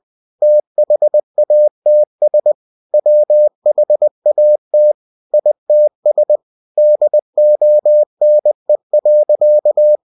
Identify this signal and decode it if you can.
Morse code, heard as a keyed tone, about 20 wpm, 600 Hz. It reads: THATS WHAT ITS DONE.
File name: cw-books / winnie-the-pooh / winnie-20wpm-page-0088